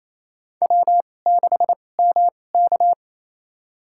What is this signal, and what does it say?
Morse code, heard as a keyed tone, about 28 wpm, spaced as for 20 wpm, 700 Hz. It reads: W6MK
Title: USA Call Signs - 4 Characters - 2x Character Spacing - 28wpm